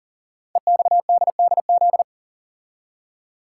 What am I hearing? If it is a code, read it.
Morse code, heard as a keyed tone, about 38 wpm, 700 Hz. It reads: EXDDZ
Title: Code Groups - 5 Letters - 40wpm